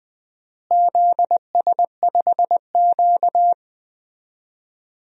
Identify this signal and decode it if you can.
Morse code, heard as a keyed tone, about 20 wpm, 700 Hz. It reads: ZS5Q
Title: DXCC-100 Call Signs - 4 Characters - Repeated 2x - 20wpm